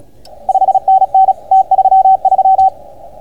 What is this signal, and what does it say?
Morse code, heard as a keyed tone, about 36 wpm, 760 Hz. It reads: 5NN T33